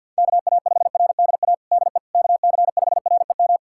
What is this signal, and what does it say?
Morse code, heard as a keyed tone, about 51 wpm, 690 Hz. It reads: KA5RDA DE KC5REM
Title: USA Callsign DE Callsign 50wpm